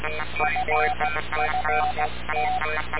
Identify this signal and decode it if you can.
Morse code, heard as a keyed tone, about 16 wpm, 740 Hz, with noise. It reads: FFT
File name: radiocap / REC z